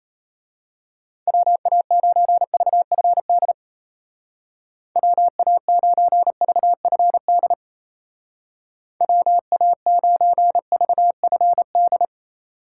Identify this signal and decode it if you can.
Morse code, heard as a keyed tone, about 32 wpm, 700 Hz. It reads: WA9VFD WA9VFD WA9VFD
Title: Speed Racing - USA 2x3 Call Signs 28wpm